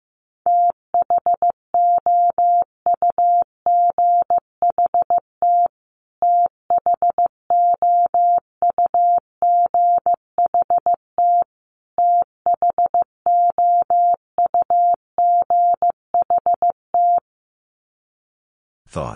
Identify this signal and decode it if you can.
Morse code, heard as a keyed tone, about 15 wpm, 700 Hz. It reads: THOUGHT THOUGHT THOUGHT
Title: Top 500 Words Repeated 15wpm